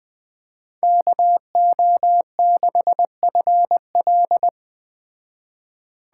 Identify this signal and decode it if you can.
Morse code, heard as a keyed tone, about 20 wpm, 700 Hz. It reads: KO6FL
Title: Speed Racing - USA 2x2 Call Signs 17wpm